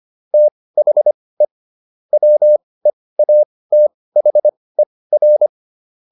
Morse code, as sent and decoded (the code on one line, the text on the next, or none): - .... / .-- . .- - .... .-.
THE WEATHER